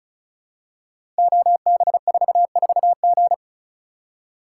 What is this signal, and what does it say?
Morse code, heard as a keyed tone, about 36 wpm, 700 Hz. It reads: OB44G
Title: Code Groups - 5 Letter:Numbers - 35wpm